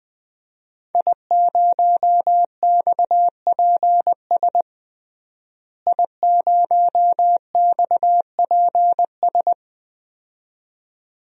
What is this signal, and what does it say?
Morse code, heard as a keyed tone, about 21 wpm, 700 Hz. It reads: I0XPS I0XPS
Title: DXCC-100 Call Signs - 5 Characters - Repeated 2x - 20wpm